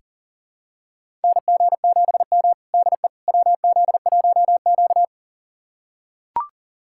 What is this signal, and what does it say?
Morse code, heard as a keyed tone, about 37 wpm, 700 Hz. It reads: NG7M DE WZ1Q
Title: USA Callsign DE Callsign - 4 Characters - 40wpm